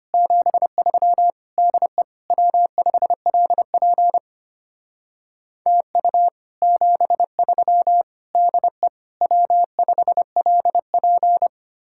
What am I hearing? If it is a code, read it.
Morse code, heard as a keyed tone, about 32 wpm, 700 Hz. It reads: 73 DE W5LP TU 73 DE W5LP